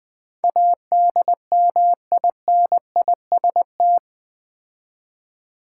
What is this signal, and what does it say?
Morse code, heard as a keyed tone, about 20 wpm, 700 Hz. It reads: ADMINIST
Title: Build A Word - 4 Letters At A Time - 10-Letter Words - 20wpm